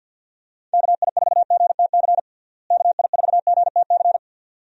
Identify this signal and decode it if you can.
Morse code, heard as a keyed tone, about 50 wpm, 700 Hz. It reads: KI4GTC KI4GTC
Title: USA Call Signs - 6 Characters - Repeated 2x - 50wpm